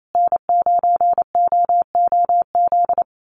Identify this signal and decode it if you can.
Morse code, heard as a keyed tone, about 28 wpm, 700 Hz. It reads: N9OOZ
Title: USA Call Signs Repeated 3x 28wpm